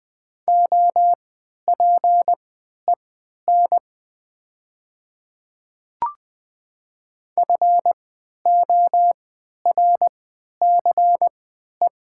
Morse code, heard as a keyed tone, about 20 wpm, spaced as for 11 wpm, 700 Hz. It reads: OPEN FORCE
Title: 3x Character Spacing - Top 500 Words - 20wpm